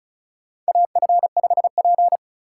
Morse code, heard as a keyed tone, about 34 wpm, 700 Hz. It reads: AF5P
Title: K1USN SST QSO Practice - 35wpm